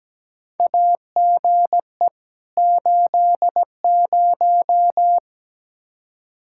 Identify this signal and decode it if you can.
Morse code, heard as a keyed tone, about 17 wpm, 700 Hz. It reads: AGE 80
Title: K1USN SST QSO Practice 17wpm